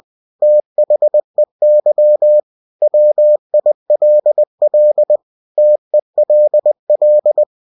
Morse code, heard as a keyed tone, about 20 wpm, 600 Hz. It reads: THEY WILL TELL